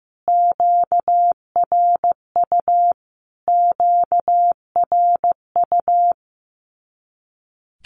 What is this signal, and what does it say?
Morse code, heard as a keyed tone, about 15 wpm, 700 Hz. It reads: QRU QRU
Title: Q-Codes Repeated 3x 15wpm